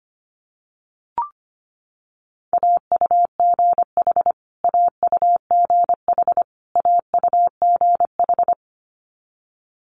Morse code, heard as a keyed tone, about 25 wpm, 700 Hz. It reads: AUGH AUGH AUGH
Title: Multi-letter Phonemes Repeated 3x 25wpm